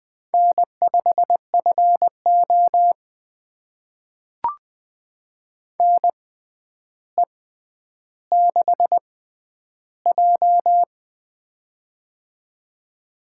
Morse code, heard as a keyed tone, about 20 wpm, spaced as for 10 wpm, 700 Hz. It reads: N5FO NE6J
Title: USA Call Signs - 6x Character Spacing - 20wpm